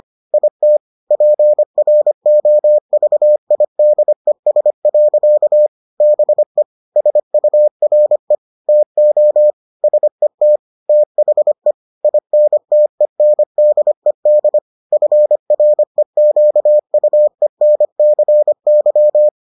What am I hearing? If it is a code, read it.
Morse code, heard as a keyed tone, about 25 wpm, 600 Hz. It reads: IT PROVIDES. BE SURE TO SET THE INTENDED FREQUENCY